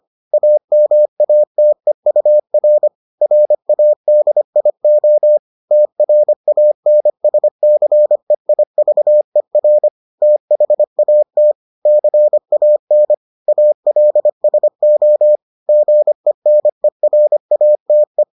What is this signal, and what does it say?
Morse code, heard as a keyed tone, about 25 wpm, 600 Hz. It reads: AMATEUR RADIO TRANSCEIVER THAT CAN ALSO GENERATE